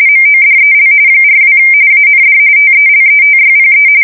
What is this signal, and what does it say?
RTTY, about 45.45 baud. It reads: MODEL 42 AND TELETYPE
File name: fsk170 / teletype-corp